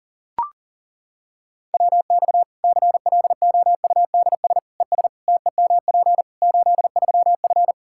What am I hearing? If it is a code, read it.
Morse code, heard as a keyed tone, about 40 wpm, 700 Hz. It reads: WX CLOUDS ES TEMP 83F
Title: K1USN SST QSO Practice 40wpm